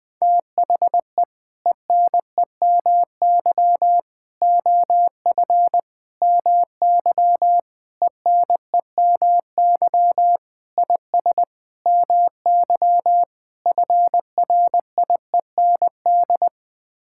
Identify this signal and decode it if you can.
Morse code, heard as a keyed tone, about 20 wpm, 700 Hz. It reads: THE ENEMY OF MY ENEMY IS MY FRIEND
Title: Sets of 8 Words - 20wpm